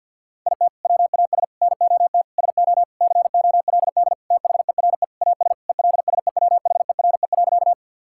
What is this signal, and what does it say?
Morse code, heard as a keyed tone, about 50 wpm, 700 Hz. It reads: IT WAS NOT SO COLD THERE AS ELSEWHERE.
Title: Sentences from Top 1600 Words 50wpm